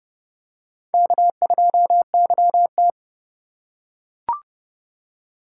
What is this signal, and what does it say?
Morse code, heard as a keyed tone, about 32 wpm, 700 Hz. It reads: K2YT